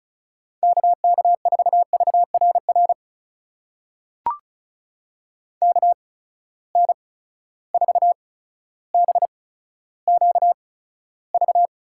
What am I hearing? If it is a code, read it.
Morse code, heard as a keyed tone, about 36 wpm, spaced as for 15 wpm, 700 Hz. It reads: KK4VRR KN4BQV
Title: USA Call Signs - 6 Characters - 8x Character Spacing - 35wpm